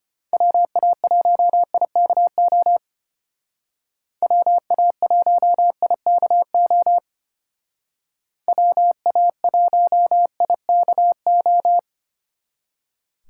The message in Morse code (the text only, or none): WA1IKO WA1IKO WA1IKO